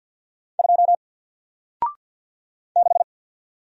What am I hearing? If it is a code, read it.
Morse code, heard as a keyed tone, about 48 wpm, 700 Hz. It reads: 2 6
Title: Single Letter-Number Alt 50wpm